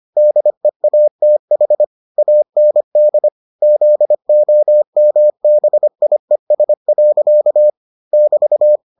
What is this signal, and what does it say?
Morse code, heard as a keyed tone, about 25 wpm, 600 Hz. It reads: DEATH AND ZOMBIES. <BT>